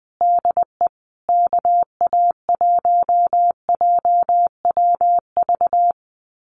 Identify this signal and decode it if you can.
Morse code, heard as a keyed tone, about 20 wpm, 700 Hz. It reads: DE KA1JWV
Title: Common QSO Phrases Encore 20wpm